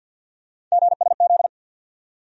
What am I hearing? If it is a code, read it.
Morse code, heard as a keyed tone, about 48 wpm, 700 Hz. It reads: GSZ
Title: USA Call Sign Postfixes - 3 Letters 50wpm